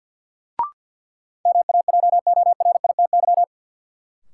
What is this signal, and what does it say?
Morse code, heard as a keyed tone, about 52 wpm, 700 Hz. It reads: MAJORITY
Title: Sets of 3 Words - New Word Drill - 50wpm